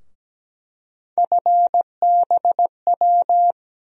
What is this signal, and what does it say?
Morse code, heard as a keyed tone, about 17 wpm, 700 Hz. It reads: FBW
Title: USA Call Sign Postfixes - 3 Letters 17wpm